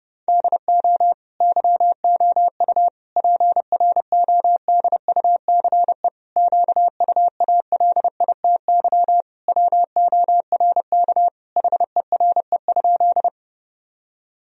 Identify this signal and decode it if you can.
Morse code, heard as a keyed tone, about 30 wpm, 700 Hz. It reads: DO YOU PRODUCE QUALITY WORK HERE?
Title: Sentences from Top 1000 Words - Review Part 4 - 30wpm